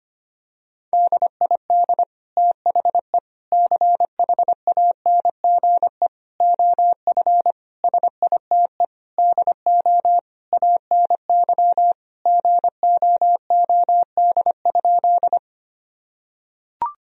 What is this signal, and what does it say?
Morse code, heard as a keyed tone, about 25 wpm, 700 Hz. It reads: DID THE CHANGE OF SITE DO ANY GOOD?
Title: Sentences from Top 1000 Words - Review Part 4 - 25wpm